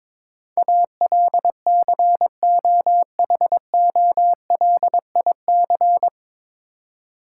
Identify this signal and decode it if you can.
Morse code, heard as a keyed tone, about 22 wpm, 700 Hz. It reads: ALCOHOLIC
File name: 9 Letter Words - First 3 Letters Given - 22wpm